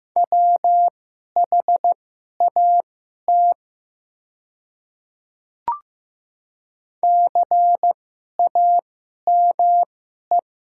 Morse code, heard as a keyed tone, about 15 wpm, spaced as for 11 wpm, 700 Hz. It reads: WHAT CAME
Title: 2x Character Spacing - Top 200 Words - 15wpm